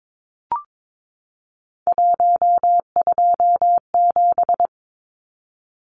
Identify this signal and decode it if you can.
Morse code, heard as a keyed tone, about 22 wpm, 700 Hz. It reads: 127
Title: Numbers - 3-Digit 22wpm